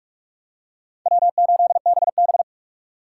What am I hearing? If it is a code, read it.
Morse code, heard as a keyed tone, about 44 wpm, 700 Hz. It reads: W8BB